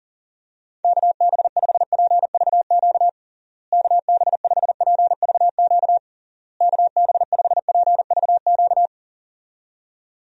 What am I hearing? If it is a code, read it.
Morse code, heard as a keyed tone, about 40 wpm, 700 Hz. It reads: KB5PVQ KB5PVQ KB5PVQ
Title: USA Call Signs - 6 Characters - Repeated 3x - 40wpm